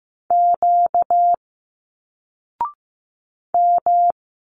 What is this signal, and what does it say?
Morse code, heard as a keyed tone, about 15 wpm, 700 Hz. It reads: Q M
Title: Single Letters 15wpm